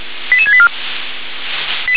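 Five-tone selective call (ZVEI), heard as quick runs of five tones, tones of about 70 ms. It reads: 88684
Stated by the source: ZVEI five-tone call received by SDR.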